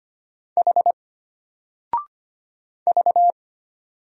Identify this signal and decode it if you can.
Morse code, heard as a keyed tone, about 25 wpm, 700 Hz. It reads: H V
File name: Single-Letter-Number-25wpm